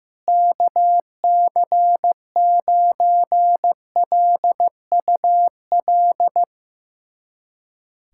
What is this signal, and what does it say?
Morse code, heard as a keyed tone, about 15 wpm, 700 Hz. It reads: KC9LUL
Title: USA Callsign DE Callsign - 6 Characters - 15wpm